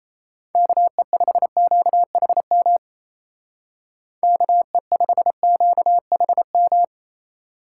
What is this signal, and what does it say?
Morse code, heard as a keyed tone, about 34 wpm, 700 Hz. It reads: KE5QHM KE5QHM
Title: Speed Racing - USA 2x3 Call Signs 28wpm